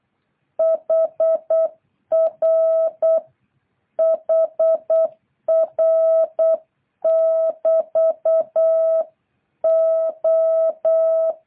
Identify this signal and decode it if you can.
Morse code, heard as a keyed tone, about 8 wpm, 650 Hz. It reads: HR HR <BT> O